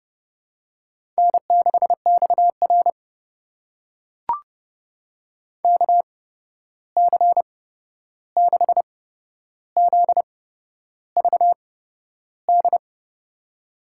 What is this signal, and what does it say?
Morse code, heard as a keyed tone, about 30 wpm, spaced as for 11 wpm, 700 Hz. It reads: N6XR KC6ZVD
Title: USA Call Signs - 8x Character Spacing - 30wpm